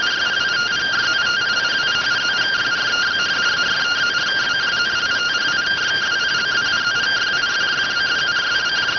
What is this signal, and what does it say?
Two-tone teleprinter signal, about 170 Hz shift.